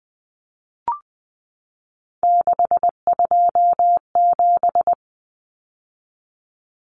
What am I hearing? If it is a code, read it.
Morse code, heard as a keyed tone, about 20 wpm, 700 Hz. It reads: 627
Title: Numbers - 3-Digit 20wpm